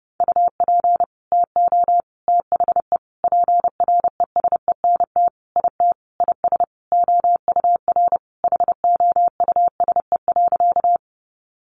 Morse code, heard as a keyed tone, about 28 wpm, 700 Hz. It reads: UP TO THE PRESENT IT IS OUR HOUSE.
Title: Sentences from Top 800 Words 30wpm